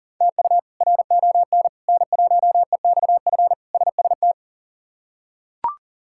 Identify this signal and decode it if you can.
Morse code, heard as a keyed tone, about 40 wpm, 700 Hz. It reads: TU RON N1EXF SST
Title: K1USN SST QSO Practice - 40wpm